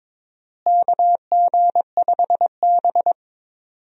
Morse code, heard as a keyed tone, about 22 wpm, 700 Hz. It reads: KG5B
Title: USA Call Signs - 4 Characters - Repeated 2x - 22wpm